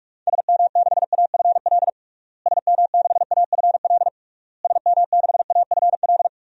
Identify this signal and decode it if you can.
Morse code, heard as a keyed tone, about 46 wpm, 700 Hz. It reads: SM6AFL SM6AFL SM6AFL
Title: DXCC-100 Call Signs - 6 Characters - Repeated 3x - 45wpm